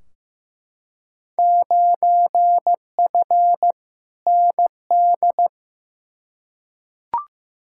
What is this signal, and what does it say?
Morse code, heard as a keyed tone, about 15 wpm, 700 Hz. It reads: 9F ND
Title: Field Day Exchanges Encore - 15wpm